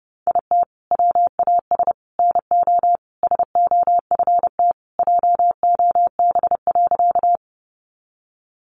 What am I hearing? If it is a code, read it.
Morse code, heard as a keyed tone, about 30 wpm, 700 Hz. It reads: IT WAS NO SOFT JOB.
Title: Sentences from Top 1500 Words 30wpm